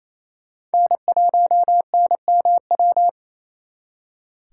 Morse code, heard as a keyed tone, about 34 wpm, 700 Hz. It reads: N1NMW
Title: Speed Racing - USA 1x3 Call Signs 28wpm